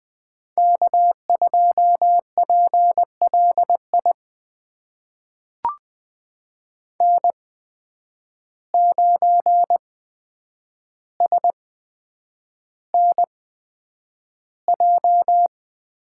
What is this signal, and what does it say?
Morse code, heard as a keyed tone, about 20 wpm, spaced as for 9 wpm, 700 Hz. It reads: K2PLI N9SNJ
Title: USA Call Signs - 5 Characters - 8x Character Spacing - 20wpm